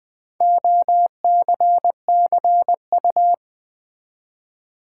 Morse code, heard as a keyed tone, about 20 wpm, 700 Hz. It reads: OCCU
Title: Build A Word - 4 Letters At A Time - 10-Letter Words - 20wpm